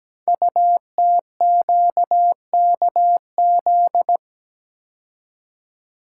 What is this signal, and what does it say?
Morse code, heard as a keyed tone, about 17 wpm, 700 Hz. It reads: UTQKZ
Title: Code Groups - 5 Letters - 17wpm